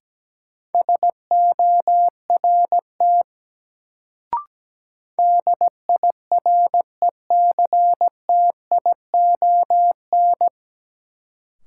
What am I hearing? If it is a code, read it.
Morse code, heard as a keyed tone, about 17 wpm, 700 Hz. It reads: SORT DIRECTION
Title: Sentences from Top 1300 Words - New Word Drill Alt - 17wpm